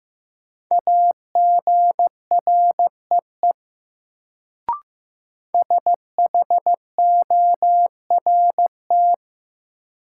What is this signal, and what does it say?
Morse code, heard as a keyed tone, about 15 wpm, 700 Hz. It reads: AGREE SHORT
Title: Sets of 5 Words Encore - New Word Drill - 15wpm